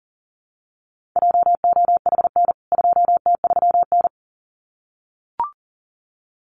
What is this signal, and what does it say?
Morse code, heard as a keyed tone, about 40 wpm, 700 Hz. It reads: JOHN 2T3N
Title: CWT Exchange 40wpm